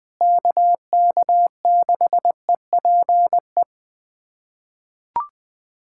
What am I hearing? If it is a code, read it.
Morse code, heard as a keyed tone, about 19 wpm, 700 Hz. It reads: KK6EPE
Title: USA Callsign DE Callsign 20wpm